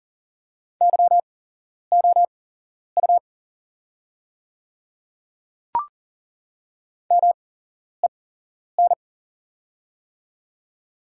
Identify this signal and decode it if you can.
Morse code, heard as a keyed tone, about 40 wpm, spaced as for 11 wpm, 700 Hz. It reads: YOU MEN
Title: 8x Character Spacing - Top 200 Words - 40wpm